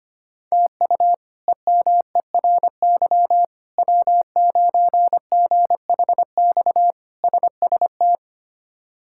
Morse code, heard as a keyed tone, about 25 wpm, 700 Hz. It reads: TU EMERY W9GHX SST
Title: K1USN SST QSO Practice - 25wpm